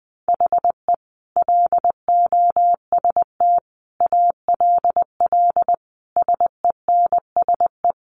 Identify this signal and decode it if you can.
Morse code, heard as a keyed tone, about 20 wpm, 700 Hz. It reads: HE LOST ALL SENSE
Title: Sentences from Top 1400 Words 20wpm